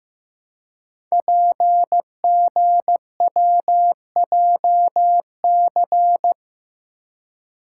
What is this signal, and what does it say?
Morse code, heard as a keyed tone, about 15 wpm, 700 Hz. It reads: PGWJC
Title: Code Groups - 5 Letters - 15wpm